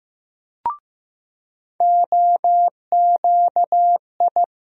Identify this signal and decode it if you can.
Morse code, heard as a keyed tone, about 15 wpm, 700 Hz. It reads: OQI
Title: USA Call Sign Postfixes - 3 Letters 15wpm